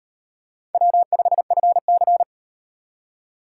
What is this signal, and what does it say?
Morse code, heard as a keyed tone, about 36 wpm, 700 Hz. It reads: W5FC